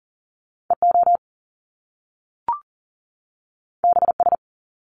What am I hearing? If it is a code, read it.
Morse code, heard as a keyed tone, about 40 wpm, 700 Hz. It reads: EO BS
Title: Top Two Letter Combinations in Words 40wpm